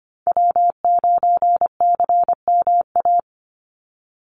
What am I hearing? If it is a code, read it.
Morse code, heard as a keyed tone, about 25 wpm, 700 Hz. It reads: W9CMA
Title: USA Call Signs 25wpm